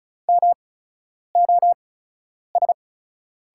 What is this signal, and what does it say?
Morse code, heard as a keyed tone, about 36 wpm, spaced as for 10 wpm, 700 Hz. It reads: MOS
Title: 8x Character Spacing - Top 100 Words - 35wpm